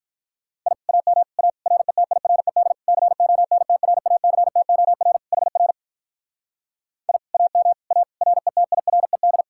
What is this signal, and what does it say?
Morse code, heard as a keyed tone, about 44 wpm, 700 Hz. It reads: I AM A RETIRED CONTRACTOR HR I AM A RETIRED